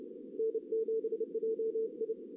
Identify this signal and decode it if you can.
Morse code, heard as a keyed tone, about 30 wpm, 440 Hz. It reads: N7JI
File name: 2016 JAN CW NAQP